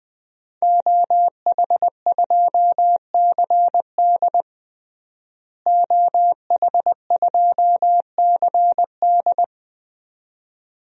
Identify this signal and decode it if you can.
Morse code, heard as a keyed tone, about 20 wpm, 700 Hz. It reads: OH2CD OH2CD
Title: DXCC-100 Call Signs - 5 Characters - Repeated 2x - 20wpm